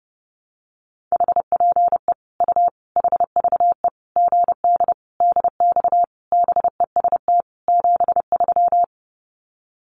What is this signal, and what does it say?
Morse code, heard as a keyed tone, about 28 wpm, 700 Hz. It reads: HPE U HVE GD DX BEST 73
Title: Common QSO Phrases 30wpm